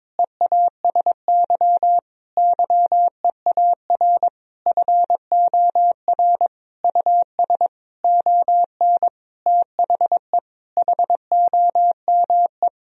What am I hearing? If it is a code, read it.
Morse code, heard as a keyed tone, about 22 wpm, 700 Hz. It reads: EASY YEAR FOR US ON THE HOME
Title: Sentences from Top 1000 Words - Review Part 2 - 22wpm